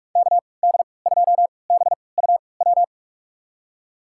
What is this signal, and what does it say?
Morse code, heard as a keyed tone, about 46 wpm, spaced as for 26 wpm, 700 Hz. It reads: KD2BUW
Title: USA Call Signs - 6 Characters - 3x Character Spacing - 45wpm